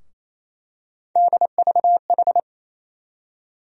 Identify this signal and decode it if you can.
Morse code, heard as a keyed tone, about 28 wpm, 700 Hz. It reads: DVH
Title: USA Call Sign Postfixes - 3 Letters 28wpm